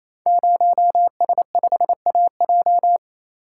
Morse code, heard as a keyed tone, about 28 wpm, 700 Hz. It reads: 0S5AJ
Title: Code Groups - 5 Letter:Numbers - 28wpm